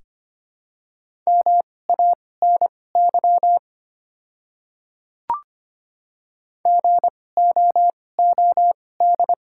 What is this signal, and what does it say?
Morse code, heard as a keyed tone, about 25 wpm, spaced as for 18 wpm, 700 Hz. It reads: MANY GOOD
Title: 2x Character Spacing - Top 200 Words - 25wpm